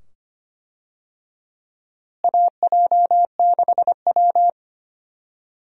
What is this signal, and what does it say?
Morse code, heard as a keyed tone, about 25 wpm, 700 Hz. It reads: AJ6W
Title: USA Call Signs - 4 Characters - Repeated 2x - 25wpm